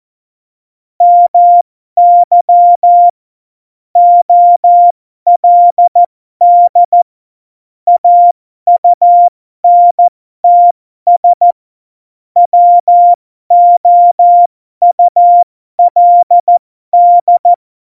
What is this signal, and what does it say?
Morse code, heard as a keyed tone, about 14 wpm, spaced as for 12 wpm, 700 Hz. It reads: MY OLD AUNTS WOULD
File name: Morsekurs / Witze